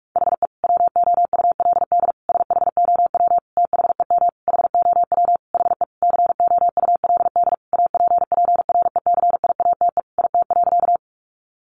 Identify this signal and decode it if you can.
Morse code, heard as a keyed tone, about 44 wpm, 700 Hz. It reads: HE WOULD SHOW THEM HOW HE COULD APPRECIATE IT.
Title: Sentences from Top 1300 Words 45wpm